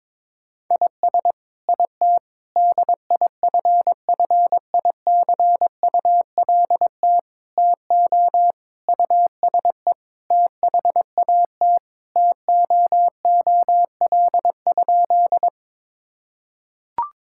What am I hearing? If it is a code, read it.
Morse code, heard as a keyed tone, about 22 wpm, 700 Hz. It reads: IS IT DIFFICULT TO USE THAT TOOL?
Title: Sentences from Top 1000 Words - Review Part 6 - 22wpm